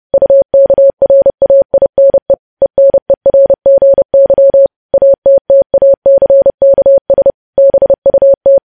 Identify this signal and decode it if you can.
Morse code, heard as a keyed tone, about 30 wpm, 550 Hz. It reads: UKRAINE ENERGY ATTACKS BUT
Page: news headlines in morse code at 30 wpm